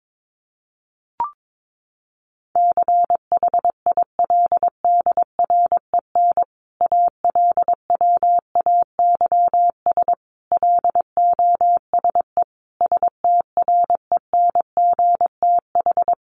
Morse code, heard as a keyed tone, about 22 wpm, 700 Hz. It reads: CHILDREN ALWAYS LOSE STRENGTH
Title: Sentences from Top 1300 Words 22wpm